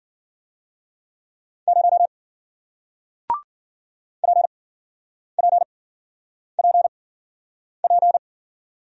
Morse code, heard as a keyed tone, about 52 wpm, 700 Hz. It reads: 0 P P P P